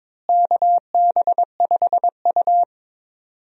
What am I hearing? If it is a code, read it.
Morse code, heard as a keyed tone, about 21 wpm, 700 Hz. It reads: KB5U
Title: USA Call Signs - 4 Characters - Repeated 3x - 22wpm